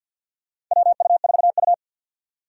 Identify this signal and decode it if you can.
Morse code, heard as a keyed tone, about 52 wpm, 700 Hz. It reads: WU4U
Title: USA Call Signs - 4 Characters - 3x Character Spacing - 50wpm